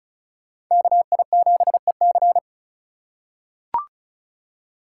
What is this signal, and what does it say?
Morse code, heard as a keyed tone, about 34 wpm, 700 Hz. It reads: KI7EC
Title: USA Call Signs - 5 Characters - 3x Character Spacing - 35wpm